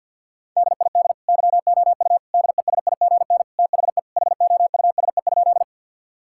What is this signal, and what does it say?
Morse code, heard as a keyed tone, about 51 wpm, 700 Hz. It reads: DID YOU DESIGN THE HOUSE?
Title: Sentences from Top 1000 Words - Review Part 5 - 50wpm